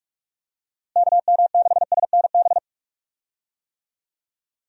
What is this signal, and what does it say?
Morse code, heard as a keyed tone, about 46 wpm, 700 Hz. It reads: KM6SNB
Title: USA Call Signs - 6 Characters - Repeated 2x - 45wpm